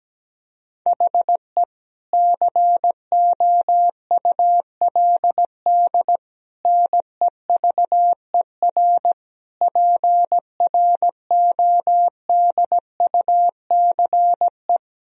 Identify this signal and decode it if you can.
Morse code, heard as a keyed tone, about 17 wpm, 700 Hz. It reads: HE COULD NEVER PRODUCE
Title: Sentences from Top 1000 Words - Review Part 3 - 17wpm